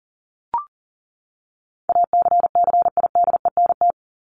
Morse code, heard as a keyed tone, about 40 wpm, 700 Hz. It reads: ACCIDENT
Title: Top 1751-2000 Words 40wpm